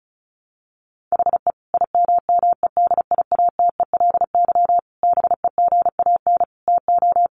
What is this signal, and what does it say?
Morse code, heard as a keyed tone, about 34 wpm, 700 Hz. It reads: HE IMMEDIATELY BEGAN TO
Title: Sentences from Top 1100 Words 35wpm